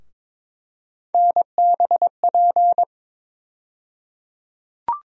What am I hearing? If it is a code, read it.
Morse code, heard as a keyed tone, about 22 wpm, 700 Hz. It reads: NBP
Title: USA Call Sign Postfixes - 3 Letters 22wpm